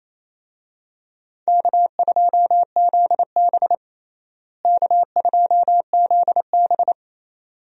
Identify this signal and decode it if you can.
Morse code, heard as a keyed tone, about 28 wpm, 700 Hz. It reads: K2ZB K2ZB
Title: USA Call Signs - 4 Characters - Repeated 3x - 28wpm